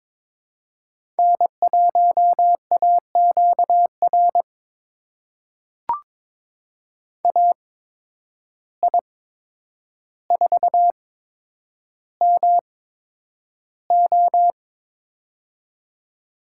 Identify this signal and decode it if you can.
Morse code, heard as a keyed tone, about 22 wpm, spaced as for 9 wpm, 700 Hz. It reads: N1AQR AI4MO
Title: USA Call Signs - 8x Character Spacing - 22wpm